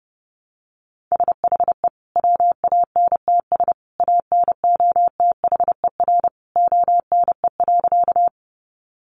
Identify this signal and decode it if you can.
Morse code, heard as a keyed tone, about 30 wpm, 700 Hz. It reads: SHE WANTS ANOTHER ONE.
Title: Easy Phrases 30wpm